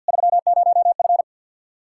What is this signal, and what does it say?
Morse code, heard as a keyed tone, about 49 wpm, 700 Hz. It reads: 30F
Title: Practice QSO - Exchange 2 - Info - 50wpm